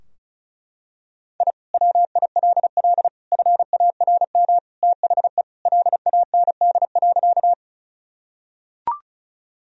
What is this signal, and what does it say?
Morse code, heard as a keyed tone, about 36 wpm, 700 Hz. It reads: I WILL FARM THE LAND.
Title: Sentences from Top 1000 Words - Review Part 2 - 35wpm